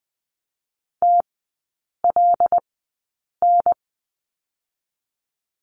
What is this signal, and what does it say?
Morse code, heard as a keyed tone, about 20 wpm, spaced as for 8 wpm, 700 Hz. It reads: TLN
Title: 166 Lesson 27e - Number 7 - Three Characters 20wpm